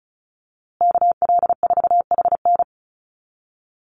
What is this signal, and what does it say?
Morse code, heard as a keyed tone, about 34 wpm, 700 Hz. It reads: KL4HN
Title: USA Call Signs Repeated 3x 35wpm